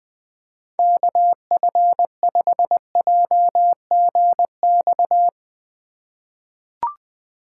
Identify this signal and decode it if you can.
Morse code, heard as a keyed tone, about 20 wpm, 700 Hz. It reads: KF5JGX
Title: USA Call Signs - 2x Character Spacing - 20wpm